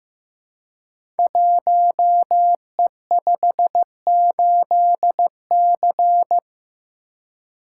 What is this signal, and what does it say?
Morse code, heard as a keyed tone, about 15 wpm, 700 Hz. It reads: 1E58C
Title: Code Groups - 5 Letter:Numbers - 15wpm